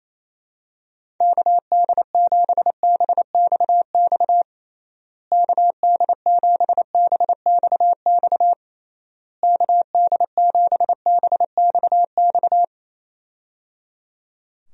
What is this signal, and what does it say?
Morse code, heard as a keyed tone, about 28 wpm, 700 Hz. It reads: KD7BXX KD7BXX KD7BXX
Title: USA Call Signs - 6 Characters - Repeated 3x - 28wpm